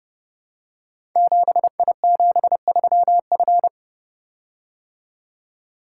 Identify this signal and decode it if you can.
Morse code, heard as a keyed tone, about 30 wpm, 700 Hz. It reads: 7I73F
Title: Code Groups - 5 Letter:Numbers - 30wpm